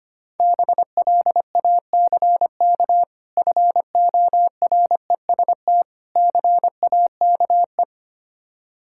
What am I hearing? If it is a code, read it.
Morse code, heard as a keyed tone, about 25 wpm, 690 Hz. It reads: BLACK FOREST CAKE
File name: practice / 25wpm / Difficult Words with Hints - 25wpm